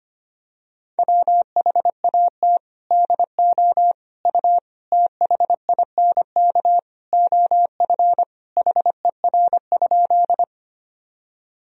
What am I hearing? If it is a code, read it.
Morse code, heard as a keyed tone, about 25 wpm, 700 Hz. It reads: WHAT DO U THINK OF HER?
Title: Sentences from Top 200 Words and Abbreviations 25wpm